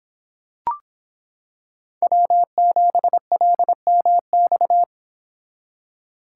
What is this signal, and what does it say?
Morse code, heard as a keyed tone, about 26 wpm, 700 Hz. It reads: W7LMX